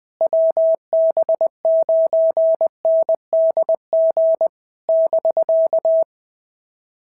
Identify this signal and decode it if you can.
Morse code, heard as a keyed tone, about 20 wpm, 650 Hz. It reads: WB9NDG <BK>
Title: POTA QSO Practice - 20wpm